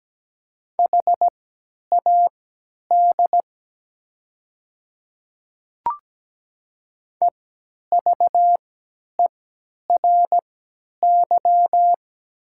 Morse code, heard as a keyed tone, about 17 wpm, spaced as for 10 wpm, 700 Hz. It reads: HAD EVERY
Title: 3x Character Spacing - Top 200 Words - 17wpm